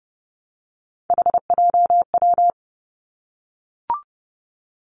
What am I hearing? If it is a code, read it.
Morse code, heard as a keyed tone, about 30 wpm, 700 Hz. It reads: HJW
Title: Three Challenging Letter-Number Combinations 30wpm